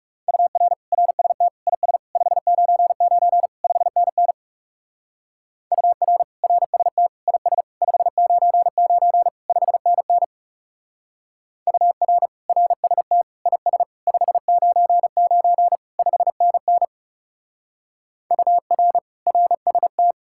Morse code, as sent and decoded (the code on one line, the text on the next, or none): ..- .-. / .-. ... - / .. ... / ..... ----. ----. / ..... -. -. / ..- .-. / .-. ... - / .. ... / ..... ----. ----. / ..... -. -. / ..- .-. / .-. ... - / .. ... / ..... ----. ----. / ..... -. -. / ..- .-. / .-. ... -
UR RST IS 599 5NN UR RST IS 599 5NN UR RST IS 599 5NN UR RST